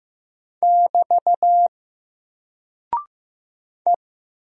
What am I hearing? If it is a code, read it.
Morse code, heard as a keyed tone, about 15 wpm, 700 Hz. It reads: <BT> E
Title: Single Character (Letter:Number:Punctuation:Prosign) 15wpm